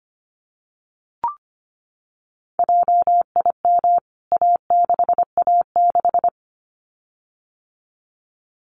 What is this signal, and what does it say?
Morse code, heard as a keyed tone, about 25 wpm, 700 Hz. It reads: JIM A6A6
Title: CWT Exchange 25wpm